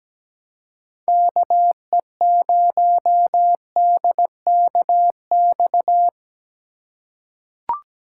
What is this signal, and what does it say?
Morse code, heard as a keyed tone, about 17 wpm, 700 Hz. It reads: KE0DKX
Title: USA Call Signs - 4x Character Spacing - 17wpm